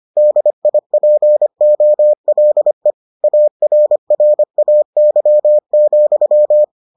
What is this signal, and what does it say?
Morse code, heard as a keyed tone, about 25 wpm, 600 Hz. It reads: DIPOLE ARRAY,